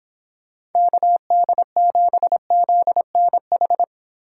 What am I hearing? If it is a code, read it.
Morse code, heard as a keyed tone, about 30 wpm, 700 Hz. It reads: KD7ZNH